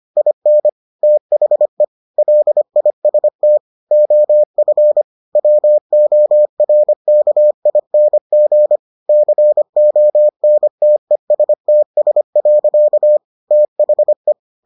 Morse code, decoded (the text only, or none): IN THE LIST OF WORKING CONTESTS. THE